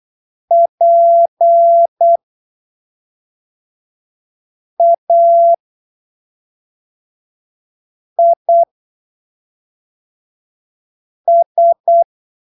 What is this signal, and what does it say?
Morse code, heard as a keyed tone, about 8 wpm, spaced as for 3 wpm, 680 Hz. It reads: PAIS
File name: arquivos / texto / texto 08 08ppm